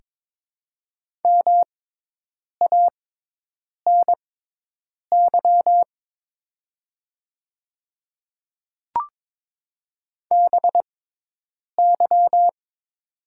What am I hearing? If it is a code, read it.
Morse code, heard as a keyed tone, about 21 wpm, spaced as for 8 wpm, 690 Hz. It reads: MANY BY
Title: 6x Character Spacing - Top 100 Words - 22wpm